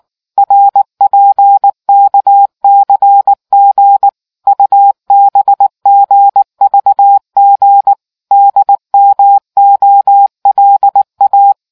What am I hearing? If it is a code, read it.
Morse code, heard as a keyed tone, about 19 wpm, 800 Hz. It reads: RPKCG UBGVG DMOLA